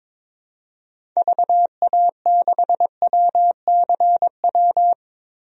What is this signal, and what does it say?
Morse code, heard as a keyed tone, about 22 wpm, 700 Hz. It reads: VA6WCW
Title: DXCC-100 Call Signs - 6 Characters - 22wpm